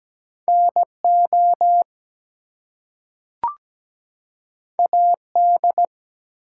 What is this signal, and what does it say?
Morse code, heard as a keyed tone, about 17 wpm, 700 Hz. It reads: NO AD
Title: Two Letter Words - 17wpm